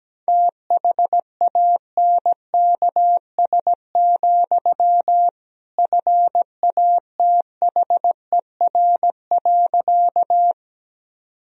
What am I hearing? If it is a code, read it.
Morse code, heard as a keyed tone, about 17 wpm, 700 Hz. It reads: THANKS, FATHER.
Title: Sentences from Top 1000 Words - Review Part 3 - 17wpm